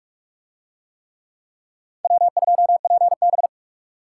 Morse code, heard as a keyed tone, about 44 wpm, 700 Hz. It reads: W2PB